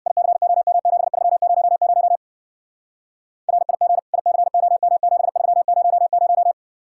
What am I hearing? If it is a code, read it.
Morse code, heard as a keyed tone, about 57 wpm, 700 Hz. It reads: ICOM7300 RIG ICOM7300
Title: Speed-Racing Common QSO Phrases - 50wpm